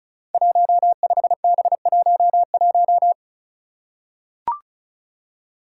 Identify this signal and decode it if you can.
Morse code, heard as a keyed tone, about 36 wpm, 700 Hz. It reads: 15B11
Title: Five Challenging Letter-Number Combinations 35wpm